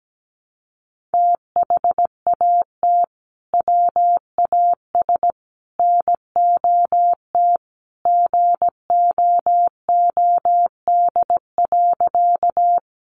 Morse code, heard as a keyed tone, about 17 wpm, 700 Hz. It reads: THAT WAS NOT GOOD.
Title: Sentences from Top 600 Words 17wpm